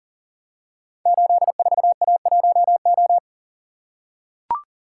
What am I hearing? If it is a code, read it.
Morse code, heard as a keyed tone, about 42 wpm, 690 Hz. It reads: 84A1O
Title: Code Groups - 5 Letter:Numbers - 40wpm